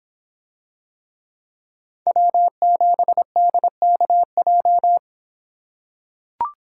Morse code, heard as a keyed tone, about 26 wpm, 700 Hz. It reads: W7DKJ